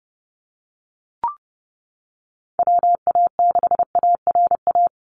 Morse code, heard as a keyed tone, about 30 wpm, 700 Hz. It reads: WA6ARA
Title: NA SOTA Chaser Callsigns 30wpm